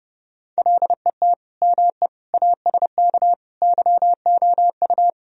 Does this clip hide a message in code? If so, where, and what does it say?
Yes: Morse code from 0.6 to 5.1 s, heard as a keyed tone, about 30 wpm, 700 Hz. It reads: LET ME ASK YOU